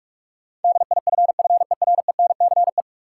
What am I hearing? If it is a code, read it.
Morse code, heard as a keyed tone, about 46 wpm, 700 Hz. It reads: DIFFERENCE
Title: Sentences from Top 800 Words - New Word Drill - 45wpm